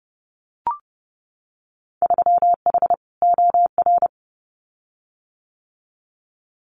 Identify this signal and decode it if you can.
Morse code, heard as a keyed tone, about 32 wpm, 700 Hz. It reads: 3H OR
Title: Winter Field Day Exchanges Encore 30wpm